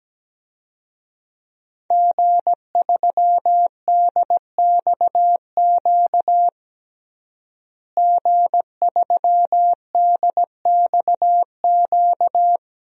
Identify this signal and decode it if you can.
Morse code, heard as a keyed tone, about 17 wpm, 700 Hz. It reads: G3DXQ G3DXQ
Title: DXCC-100 Call Signs - 5 Characters - Repeated 3x - 17wpm